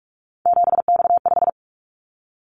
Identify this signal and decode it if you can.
Morse code, heard as a keyed tone, about 46 wpm, 700 Hz. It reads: 7X5
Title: Callsign Prefixes for Top 100 DXCC - 45wpm